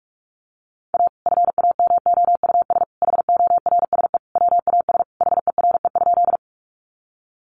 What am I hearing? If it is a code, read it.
Morse code, heard as a keyed tone, about 44 wpm, 700 Hz. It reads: A FAMOUS HORSE WAS HERE?
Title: Sentences from Top 1500 Words 45wpm